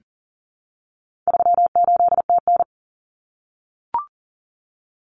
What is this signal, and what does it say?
Morse code, heard as a keyed tone, about 40 wpm, 700 Hz. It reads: 38TN
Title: Common Cut Numbers - 4-Digit 40wpm